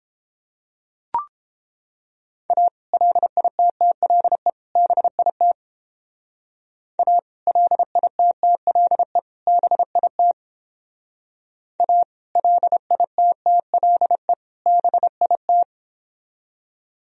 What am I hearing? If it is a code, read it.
Morse code, heard as a keyed tone, about 29 wpm, 700 Hz. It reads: A LITTLE BIT A LITTLE BIT A LITTLE BIT